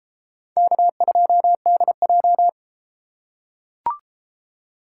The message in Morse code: K2DJ

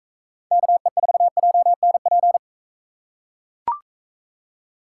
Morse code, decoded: KE4JNP

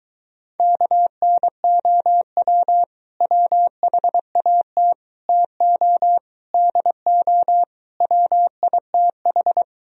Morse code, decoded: KNOW WHAT TO DO WITH

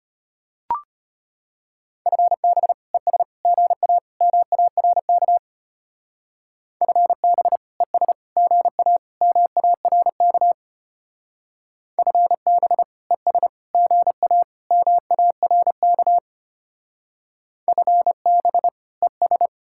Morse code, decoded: FB ES GA MARK FB ES GA MARK FB ES GA MARK FB ES

